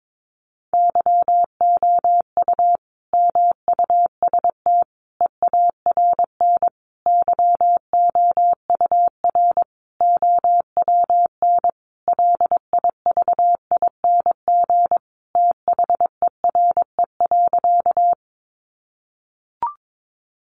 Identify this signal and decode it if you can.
Morse code, heard as a keyed tone, about 22 wpm, 700 Hz. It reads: YOU MUST EARN YOUR OWN LIVING THERE.
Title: Sentences from Top 1500 Words 22wpm